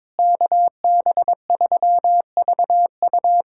Morse code, heard as a keyed tone, about 22 wpm, 700 Hz. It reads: KB3VU
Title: USA Callsign DE Callsign - 5 Characters - 22wpm